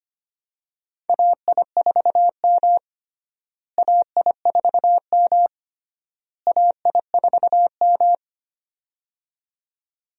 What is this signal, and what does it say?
Morse code, heard as a keyed tone, about 25 wpm, 700 Hz. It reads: AI4M AI4M AI4M
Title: USA Call Signs - 4 Characters - Repeated 3x - 25wpm